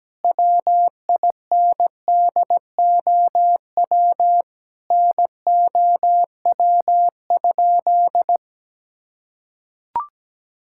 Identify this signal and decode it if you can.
Morse code, heard as a keyed tone, about 17 wpm, 700 Hz. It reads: WINDOW NOW?
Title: Sentences from Top 1600 Words 17wpm